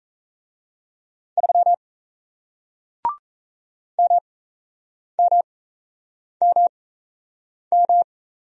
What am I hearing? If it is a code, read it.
Morse code, heard as a keyed tone, about 38 wpm, spaced as for 8 wpm, 700 Hz. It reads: 3 MMMM